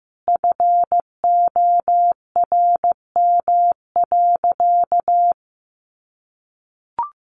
Morse code, heard as a keyed tone, about 15 wpm, 700 Hz. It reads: FORM.
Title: Sentences from Top 1500 Words 15wpm